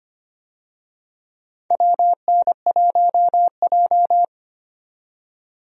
Morse code, heard as a keyed tone, about 25 wpm, 700 Hz. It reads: WN1J
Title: USA Call Signs - 4 Characters - 25wpm